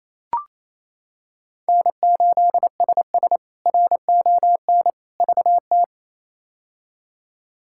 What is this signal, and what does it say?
Morse code, heard as a keyed tone, about 28 wpm, 700 Hz. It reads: N8SS RON VT
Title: K1USN SST QSO Practice - 28wpm